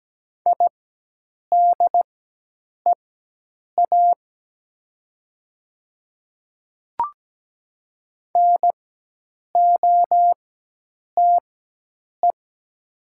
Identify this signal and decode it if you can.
Morse code, heard as a keyed tone, about 17 wpm, spaced as for 8 wpm, 700 Hz. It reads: IDEA NOTE
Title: 4x Character Spacing - Top 500 Words - 17wpm